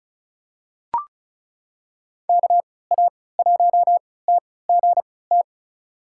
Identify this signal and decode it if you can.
Morse code, heard as a keyed tone, about 34 wpm, spaced as for 20 wpm, 700 Hz. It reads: KA1TGT
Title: USA Call Signs - 6 Characters - 3x Character Spacing - 35wpm